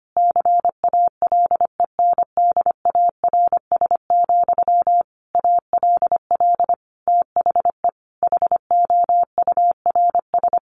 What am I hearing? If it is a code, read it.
Morse code, heard as a keyed tone, about 25 wpm, 700 Hz. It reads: CALENDARS, ALL THE HOURS
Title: A Sound Of Thunder 25wpm